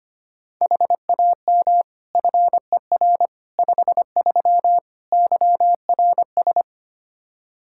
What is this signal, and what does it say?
Morse code, heard as a keyed tone, about 25 wpm, 700 Hz. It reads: HAM FER 53 YRS
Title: Practice QSO - Exchange 3 - More Info - 25wpm